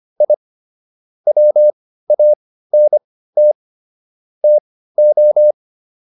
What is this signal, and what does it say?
Morse code, heard as a keyed tone, about 25 wpm, spaced as for 15 wpm, 600 Hz. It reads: I WANT TO